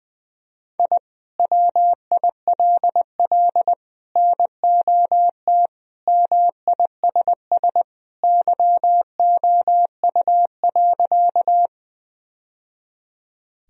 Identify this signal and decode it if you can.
Morse code, heard as a keyed tone, about 20 wpm, 700 Hz. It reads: I WILL NOT MISS YOU.
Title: Sentences from Top 1000 Words - Review Part 3 - 20wpm